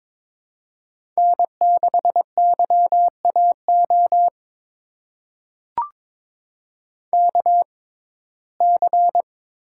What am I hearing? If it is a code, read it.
Morse code, heard as a keyed tone, about 22 wpm, spaced as for 16 wpm, 700 Hz. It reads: N6YAO KC
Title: USA Call Signs - 5 Characters - 6x Character Spacing - 22wpm